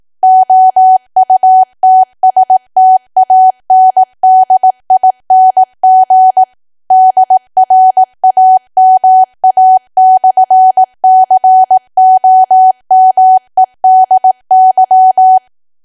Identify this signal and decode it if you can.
Morse code, heard as a keyed tone, about 18 wpm, 750 Hz. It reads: OUTSTANDING DRAMA/COMEDY
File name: Archive / 18 WPM / 18WPM